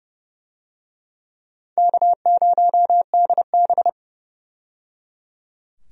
Morse code, heard as a keyed tone, about 29 wpm, 700 Hz. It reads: K0DB
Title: USA Call Signs - 4 Characters - 30wpm